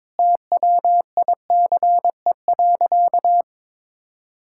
Morse code, heard as a keyed tone, about 22 wpm, 700 Hz. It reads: TWICE.
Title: Sentences from Top 1600 Words 22wpm